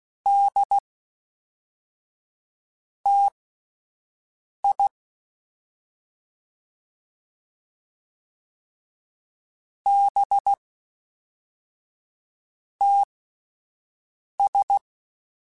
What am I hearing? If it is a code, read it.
Morse code, heard as a keyed tone, about 16 wpm, spaced as for 4 wpm, 800 Hz. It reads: DTI BTS
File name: CW Comparisons / CW Comparisons-mp3s / CW Comparisons Broken Characters @10